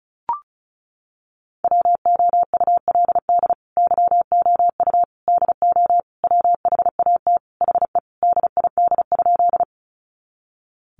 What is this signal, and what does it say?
Morse code, heard as a keyed tone, about 34 wpm, 700 Hz. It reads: WOULD YOU DO WHAT HE DID?
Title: Sentences from Top 100 Words 35wpm